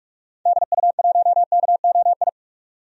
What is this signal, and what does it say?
Morse code, heard as a keyed tone, about 41 wpm, 690 Hz. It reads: DU1KOI
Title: DXCC-100 Call Signs - 6 Characters - 45wpm